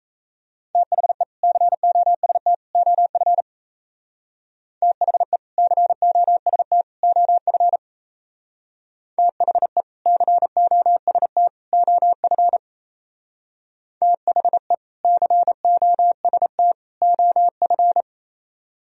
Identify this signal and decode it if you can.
Morse code, heard as a keyed tone, about 37 wpm, 700 Hz. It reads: THE COST OF THE COST OF THE COST OF THE COST OF